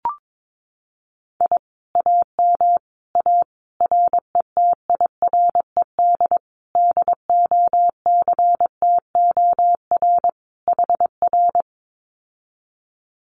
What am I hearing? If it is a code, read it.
Morse code, heard as a keyed tone, about 22 wpm, 700 Hz. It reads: I AM A RETIRED DOCTOR HR
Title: Common QSO Phrases Encore 22wpm